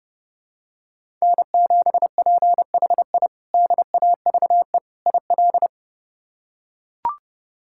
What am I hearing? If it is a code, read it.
Morse code, heard as a keyed tone, about 30 wpm, 700 Hz. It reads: N7PHI DAVE IL
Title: K1USN SST QSO Practice - 30wpm